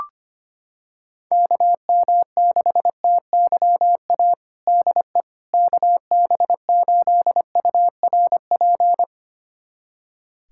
Morse code, heard as a keyed tone, about 25 wpm, 700 Hz. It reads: KM6TYA DE KB8URP
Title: USA Callsign DE Callsign 25wpm